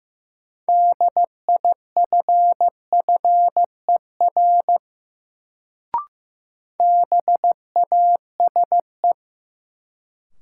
Morse code, heard as a keyed tone, about 15 wpm, 700 Hz. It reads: DIFFER BASE
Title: Sentences from Top 1000 Words - Review Part 1 - New Word Drill - 15wpm